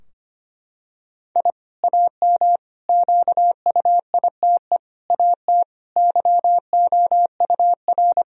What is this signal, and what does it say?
Morse code, heard as a keyed tone, about 25 wpm, 700 Hz. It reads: I AM QUITE AT YOUR
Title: Sentences from Top 1000 Words - Review Part 4 - 25wpm